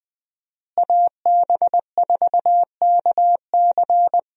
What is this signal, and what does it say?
Morse code, heard as a keyed tone, about 20 wpm, 700 Hz. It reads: AB4KC
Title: USA Call Signs - 5 Characters - Repeated 2x - 20wpm